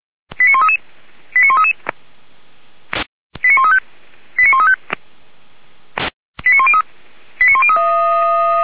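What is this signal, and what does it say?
Five-tone selective call (ZVEI), heard as quick runs of five tones, tones of about 70 ms. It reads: 89133 89133 89135 89135 89103 89103
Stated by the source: ZVEI five-tone call received by SDR, from the feuerwehr-orlamuende.org fire brigade audio archive.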